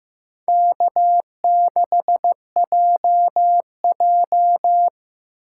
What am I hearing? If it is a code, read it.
Morse code, heard as a keyed tone, about 18 wpm, 700 Hz. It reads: K6JJ
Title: Speed Racing - USA Call Signs 15wpm